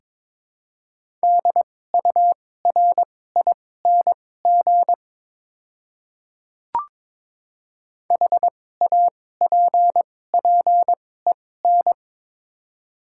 Morse code, heard as a keyed tone, about 22 wpm, spaced as for 16 wpm, 700 Hz. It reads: DURING HAPPEN
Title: 2x Character Spacing - Top 500 Words - 22wpm